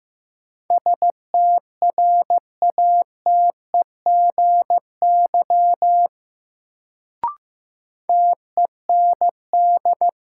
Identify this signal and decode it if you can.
Morse code, heard as a keyed tone, about 15 wpm, 700 Hz. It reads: STRATEGY TEND
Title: Sentences from Top 1000 Words - Review Part 5 - New Word Drill - 15wpm